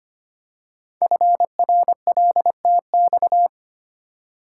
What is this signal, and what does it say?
Morse code, heard as a keyed tone, about 25 wpm, 700 Hz. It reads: FRLTX
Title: Code Groups - 5 Letters - 25wpm